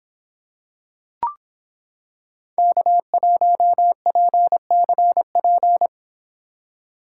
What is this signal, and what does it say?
Morse code, heard as a keyed tone, about 26 wpm, 700 Hz. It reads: K1PCP